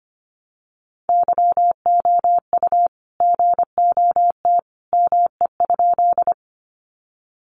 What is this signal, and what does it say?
Morse code, heard as a keyed tone, about 25 wpm, 700 Hz. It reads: YOU GOT ME?
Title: Sentences from Top 300 Words 25wpm